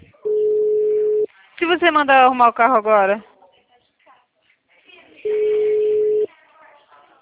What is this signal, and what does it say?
Ringback tone (Europe, 425 Hz), with speech.